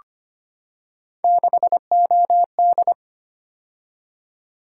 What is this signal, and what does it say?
Morse code, heard as a keyed tone, about 25 wpm, 700 Hz. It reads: 6OD
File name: Three Letter-Number Combinations - 25wpm